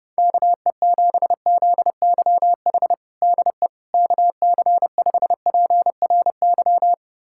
Morse code, heard as a keyed tone, about 30 wpm, 700 Hz. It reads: KE7ZYH DE KC5PRY
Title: USA Callsign DE Callsign - 6 Characters - 30wpm